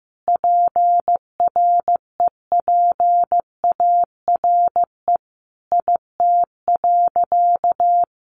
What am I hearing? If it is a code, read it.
Morse code, heard as a keyed tone, about 15 wpm, 700 Hz. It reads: PREPARE IT.
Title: Sentences from Top 1100 Words 15wpm